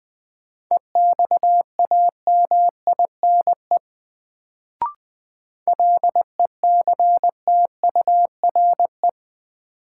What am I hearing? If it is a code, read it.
Morse code, heard as a keyed tone, about 20 wpm, 700 Hz. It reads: EXAMINE LECTURE
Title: Sentences from Top 1600 Words - New Word Drill - 20wpm